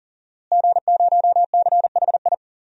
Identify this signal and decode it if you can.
Morse code, heard as a keyed tone, about 40 wpm, 700 Hz. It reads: G0CHI
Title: DXCC-100 Call Signs - 5 Characters - Repeated 2x - 40wpm